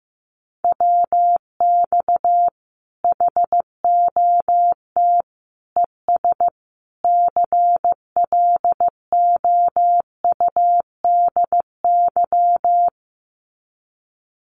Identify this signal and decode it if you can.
Morse code, heard as a keyed tone, about 15 wpm, 700 Hz. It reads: WX HOT ES CLOUDY
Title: Common QSO Phrases 15wpm